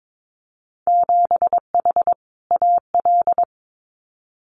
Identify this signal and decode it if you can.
Morse code, heard as a keyed tone, about 22 wpm, 700 Hz. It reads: 7H AL
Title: Winter Field Day Exchanges Encore 22wpm